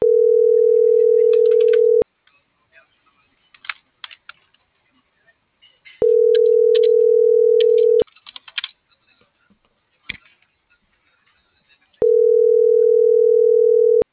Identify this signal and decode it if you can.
Ringback tone (North America), 440 + 480 Hz.